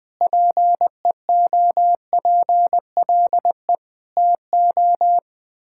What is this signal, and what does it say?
Morse code, heard as a keyed tone, about 20 wpm, 700 Hz. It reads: PEOPLE TO
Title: Sets of 6 Words - 20wpm